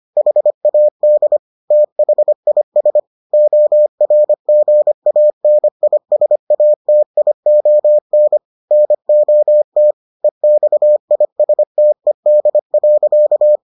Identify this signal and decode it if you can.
Morse code, heard as a keyed tone, about 25 wpm, 600 Hz. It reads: HAD THIS ORGANISATION NOT EXISTED.